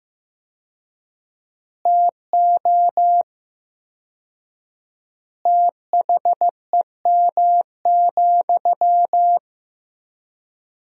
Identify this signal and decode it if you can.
Morse code, heard as a keyed tone, about 15 wpm, 700 Hz. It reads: TO THEM,
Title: 4x Word-Spacing - Sentences from Top 400 Words - 15wpm